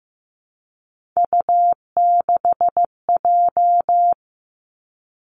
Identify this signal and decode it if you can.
Morse code, heard as a keyed tone, about 15 wpm, 700 Hz. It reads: U6J
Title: Three Challenging Letter-Number Combinations 15wpm